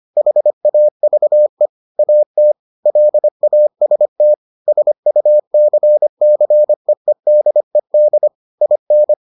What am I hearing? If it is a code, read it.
Morse code, heard as a keyed tone, about 25 wpm, 600 Hz. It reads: HAVE AT LAST SUCCEEDED IN